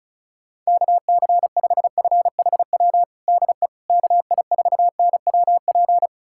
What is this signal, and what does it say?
Morse code, heard as a keyed tone, about 36 wpm, 700 Hz. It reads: KC5FHW DE KI4NWP
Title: USA Callsign DE Callsign 35wpm